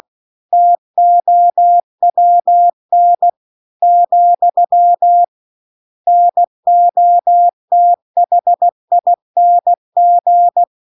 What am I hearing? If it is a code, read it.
Morse code, heard as a keyed tone, about 16 wpm, 700 Hz. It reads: TOWN , NOTHING
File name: CW-The-Big-Twoheaded-River-1-16WPM